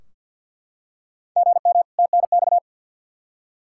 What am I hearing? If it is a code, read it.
Morse code, heard as a keyed tone, about 53 wpm, 700 Hz. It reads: GM TNX
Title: Practice QSO - Exchange 1 - Beginning - 50wpm